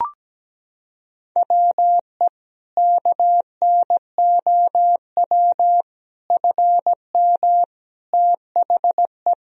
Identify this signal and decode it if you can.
Morse code, heard as a keyed tone, about 17 wpm, 700 Hz. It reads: WE KNOW FM THE
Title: Sentences from Top 500 Words and Abbreviations 17wpm